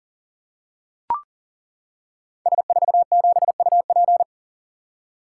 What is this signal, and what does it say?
Morse code, heard as a keyed tone, about 38 wpm, 690 Hz. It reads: S47UP
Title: Code Groups - 5 Letter:Numbers - 40wpm